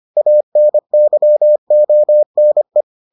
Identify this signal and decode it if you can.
Morse code, heard as a keyed tone, about 25 wpm, 600 Hz. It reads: ANYONE